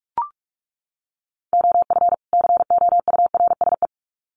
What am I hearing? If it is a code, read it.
Morse code, heard as a keyed tone, about 44 wpm, 700 Hz. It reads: OF COURSE
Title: Sets of 2 Words 45wpm